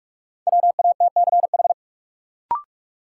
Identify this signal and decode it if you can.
Morse code, heard as a keyed tone, about 46 wpm, 700 Hz. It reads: WATCH
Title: Sentences from Top 300 Words - New Word Drill - 45wpm